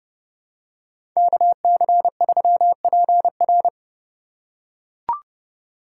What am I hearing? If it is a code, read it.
Morse code, heard as a keyed tone, about 30 wpm, 700 Hz. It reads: KC3PR
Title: USA Call Signs - 5 Characters - 2x Character Spacing - 30wpm